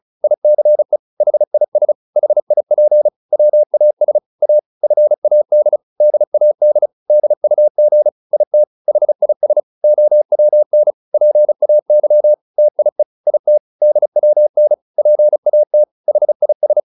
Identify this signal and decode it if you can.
Morse code, heard as a keyed tone, about 34 wpm, 600 Hz. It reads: ICE HIS HIP WAS A FAD DAD DUG IT HIS OWN PAY TIE IT DWN PAT HIS